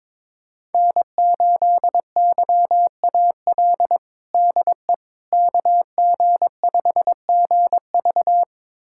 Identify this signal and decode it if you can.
Morse code, heard as a keyed tone, about 22 wpm, 700 Hz. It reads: N8YAL DE KG5GV
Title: USA Callsign DE Callsign - 5 Characters - 22wpm